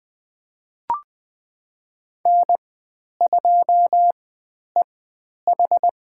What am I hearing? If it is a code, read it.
Morse code, heard as a keyed tone, about 20 wpm, spaced as for 10 wpm, 700 Hz. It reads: N2EH
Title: 109 Lesson 18g - Letter m - Call Signs 20wpm